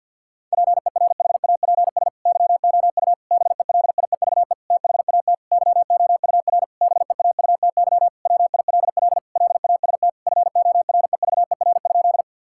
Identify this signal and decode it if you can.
Morse code, heard as a keyed tone, about 50 wpm, 700 Hz. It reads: PERHAPS YOU BELIEVE THAT YOUR BEAUTY WILL LAST FOREVER?
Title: Sentences from Top 1600 Words 50wpm